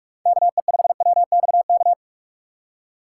Morse code, heard as a keyed tone, about 46 wpm, 700 Hz. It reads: KE5WXK
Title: USA Call Signs - 6 Characters - 2x Character Spacing - 45wpm